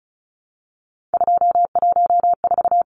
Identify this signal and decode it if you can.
Morse code, heard as a keyed tone, about 34 wpm, 700 Hz. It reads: 214
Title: Numbers - 3-Digit 35wpm